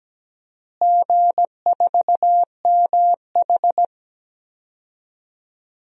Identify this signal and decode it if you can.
Morse code, heard as a keyed tone, about 17 wpm, 700 Hz. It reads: G4MH
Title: DXCC-100 Call Signs - 4 Characters - Repeated 3x - 17wpm